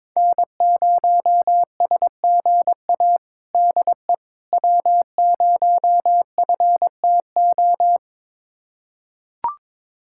Morse code, heard as a keyed tone, about 21 wpm, 700 Hz. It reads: N0SGA DE W0FTO
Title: USA Callsign DE Callsign - 5 Characters - 22wpm